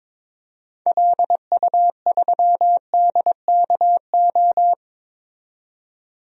Morse code, heard as a keyed tone, about 22 wpm, 700 Hz. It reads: LU3DKO
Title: DXCC-100 Call Signs - 6 Characters - 22wpm